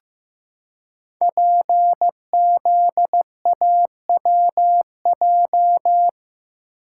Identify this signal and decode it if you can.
Morse code, heard as a keyed tone, about 15 wpm, 700 Hz. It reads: PZAWJ
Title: Code Groups - 5 Letter:Numbers - 15wpm